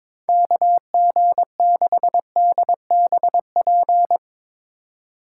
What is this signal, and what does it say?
Morse code, heard as a keyed tone, about 22 wpm, 700 Hz. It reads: KG6DBP
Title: USA Call Signs - 6 Characters - 2x Character Spacing - 22wpm